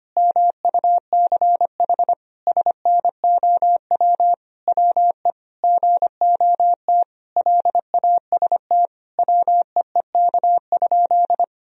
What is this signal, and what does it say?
Morse code, heard as a keyed tone, about 25 wpm, 700 Hz. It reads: MUCH SNOW WE GOT LAST WEEK?
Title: Sentences from Top 1000 Words - Review Part 3 - 25wpm